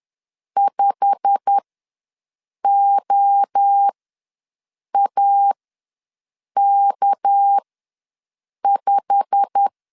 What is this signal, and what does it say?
Morse code, heard as a keyed tone, about 10 wpm, spaced as for 6 wpm, 800 Hz. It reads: 5OAK5
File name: LEZIONI / 8 YJ45 / 8 YJ45 ws 10wpm 3843